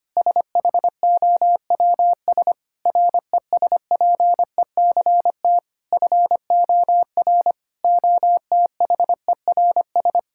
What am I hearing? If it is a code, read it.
Morse code, heard as a keyed tone, about 25 wpm, 700 Hz. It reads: SHOWS RESPECT FOR OTHERS
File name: practice / 25wpm / Difficult Sentences with Hints - 25wpm